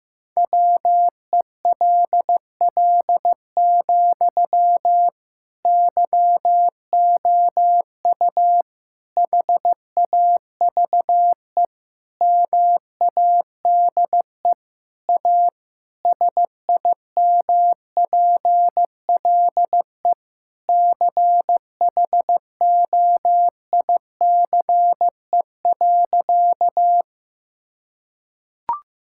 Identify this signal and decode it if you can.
Morse code, heard as a keyed tone, about 15 wpm, 700 Hz. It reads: WELL, YOU HAVE MADE A SIMPLE CHOICE.
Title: Sentences from Top 1000 Words - Review Part 6 - 15wpm